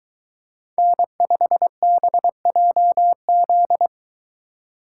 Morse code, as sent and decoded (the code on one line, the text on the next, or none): -. ..... -... .--- --..
N5BJZ